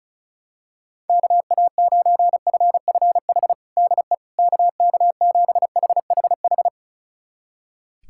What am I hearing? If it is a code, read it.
Morse code, heard as a keyed tone, about 34 wpm, 700 Hz. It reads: KA9FFH DE KK7HHH
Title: USA Callsign DE Callsign - 6 Characters - 35wpm